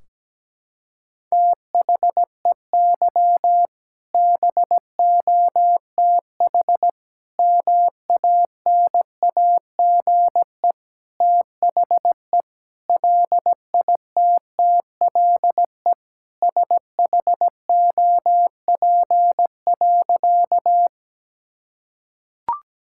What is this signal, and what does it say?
Morse code, heard as a keyed tone, about 17 wpm, 700 Hz. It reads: THEY BOTH MANAGE THE LITTLE SHOP.
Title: Sentences from Top 1600 Words 17wpm